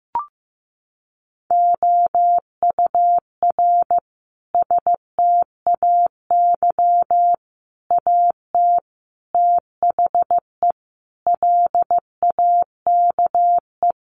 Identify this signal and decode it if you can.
Morse code, heard as a keyed tone, about 15 wpm, 700 Hz. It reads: OUR STAY AT THE LAKE
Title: Sentences from Top 1400 Words 15wpm